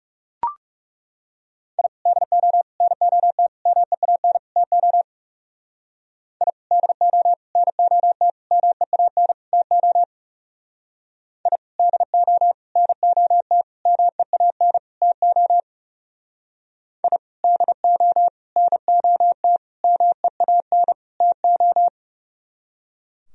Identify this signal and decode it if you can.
Morse code, heard as a keyed tone, about 36 wpm, 700 Hz. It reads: I DO NOT MEAN TO I DO NOT MEAN TO I DO NOT MEAN TO I DO NOT MEAN TO
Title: Speed-Racing - Sets of 5 Words 30wpm